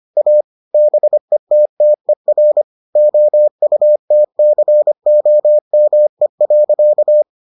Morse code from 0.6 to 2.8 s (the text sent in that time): BETTER